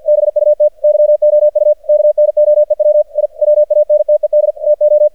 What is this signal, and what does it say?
Morse code, heard as a keyed tone, about 54 wpm, 600 Hz. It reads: BUT YOU KNOEW I WANTED TO
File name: archived / CRSnet